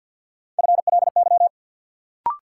Morse code, heard as a keyed tone, about 46 wpm, 700 Hz. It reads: FLY
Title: Sentences from Top 500 Words - New Word Drill - 50wpm